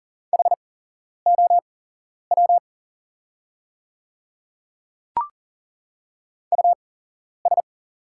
Morse code, heard as a keyed tone, about 40 wpm, spaced as for 11 wpm, 690 Hz. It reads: HOW US
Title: 8x Character Spacing - Top 200 Words - 40wpm